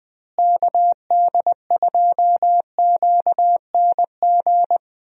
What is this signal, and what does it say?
Morse code, heard as a keyed tone, about 20 wpm, 700 Hz. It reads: KD2QNG
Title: USA Call Signs - 6 Characters - Repeated 3x - 20wpm